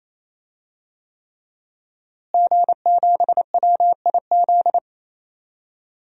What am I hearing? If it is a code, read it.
Morse code, heard as a keyed tone, about 27 wpm, 700 Hz. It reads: G7WIZ
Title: DXCC-100 Call Signs - 5 Characters - Repeated 3x - 28wpm